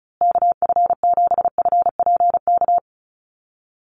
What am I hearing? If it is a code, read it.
Morse code, heard as a keyed tone, about 34 wpm, 700 Hz. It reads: KF7FPK
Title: USA Call Signs Repeated 2x 35wpm